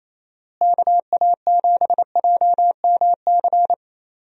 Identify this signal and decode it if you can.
Morse code, heard as a keyed tone, about 27 wpm, 700 Hz. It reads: KA7JMC
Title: USA Call Signs - 6 Characters - Repeated 2x - 28wpm